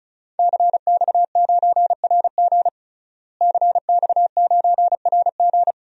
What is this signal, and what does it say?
Morse code, heard as a keyed tone, about 36 wpm, 700 Hz. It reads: CX9RG CX9RG
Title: DXCC-100 Call Signs - 5 Characters - Repeated 2x - 35wpm